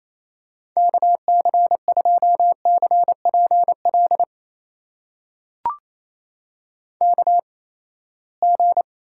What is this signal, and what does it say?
Morse code, heard as a keyed tone, about 28 wpm, spaced as for 19 wpm, 700 Hz. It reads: KC2CPL KG
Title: USA Call Signs - 6 Characters - 8x Character Spacing - 28wpm